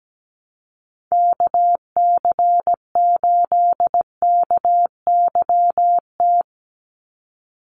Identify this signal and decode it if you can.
Morse code, heard as a keyed tone, about 17 wpm, 700 Hz. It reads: KC8KYT
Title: USA Call Signs 17wpm